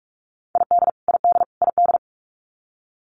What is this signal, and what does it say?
Morse code, heard as a keyed tone, about 44 wpm, 700 Hz. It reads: ID ID ID
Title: US States Repeated 3x 45wpm